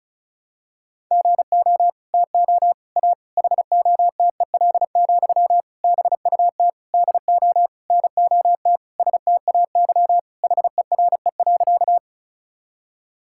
Morse code, heard as a keyed tone, about 34 wpm, 700 Hz. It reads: GO TO A HOTEL, BUT DO NOT STAY HERE.
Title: Sentences from Top 1600 Words 35wpm